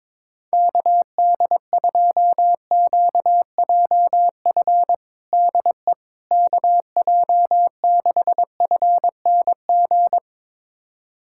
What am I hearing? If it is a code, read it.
Morse code, heard as a keyed tone, about 22 wpm, 700 Hz. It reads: KD2QJF DE KJ6FNG
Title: USA Callsign DE Callsign - 6 Characters - 22wpm